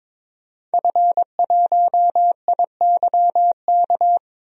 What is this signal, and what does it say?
Morse code, heard as a keyed tone, about 22 wpm, 700 Hz. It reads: F1IYK
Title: DXCC-100 Call Signs - 5 Characters - 22wpm